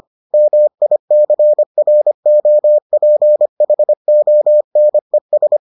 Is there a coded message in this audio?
Yes: Morse code from 0.3 to 5.6 s, heard as a keyed tone, about 25 wpm, 600 Hz. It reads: MICROPHONES